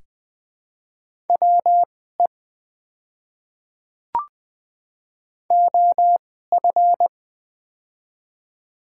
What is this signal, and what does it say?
Morse code, heard as a keyed tone, about 20 wpm, spaced as for 15 wpm, 700 Hz. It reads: WE OF
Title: 2x Character Spacing - Top 100 Words - 20wpm